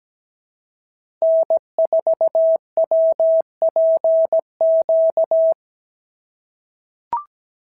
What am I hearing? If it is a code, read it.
Morse code, heard as a keyed tone, about 17 wpm, 650 Hz. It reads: N4WPQ
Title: SOTA QSO Practice 17wpm